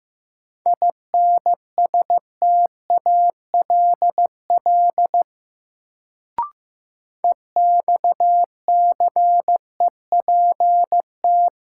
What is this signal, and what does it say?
Morse code, heard as a keyed tone, about 15 wpm, 700 Hz. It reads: INSTALL EXCEPT
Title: Sentences from Top 1600 Words - New Word Drill - 15wpm